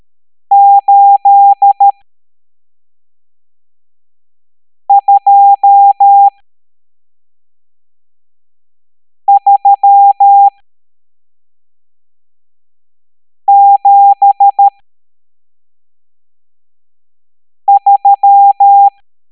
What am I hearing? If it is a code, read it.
Morse code, heard as a keyed tone, about 13 wpm, spaced as for 3 wpm, 800 Hz. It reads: 82373
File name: hamdata / L12-EX02-CHIFFRES